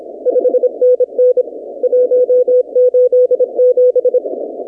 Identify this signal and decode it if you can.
Morse code, heard as a keyed tone, about 28 wpm, 500 Hz. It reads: 5NN 187
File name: 2019-cw-wpx-kt5j-k5pi